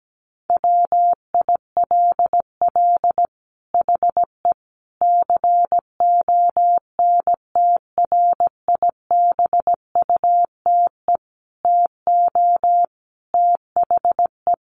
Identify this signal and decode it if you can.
Morse code, heard as a keyed tone, about 17 wpm, 700 Hz. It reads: WILL HE CONTRIBUTE TO THE
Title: Sentences from Top 1400 Words 17wpm